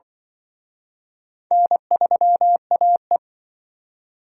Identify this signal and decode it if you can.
Morse code, heard as a keyed tone, about 25 wpm, 700 Hz. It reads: N3AE
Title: Speed Racing - USA 1x2 Call Signs 20wpm